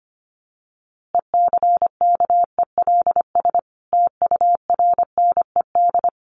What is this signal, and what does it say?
Morse code, heard as a keyed tone, about 25 wpm, 700 Hz. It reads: ECKELS TURNED
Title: A Sound Of Thunder 25wpm